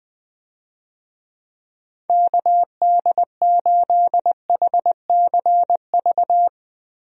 Morse code, heard as a keyed tone, about 20 wpm, 700 Hz. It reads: KD8HCV
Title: USA Call Signs - 6 Characters - Repeated 2x - 20wpm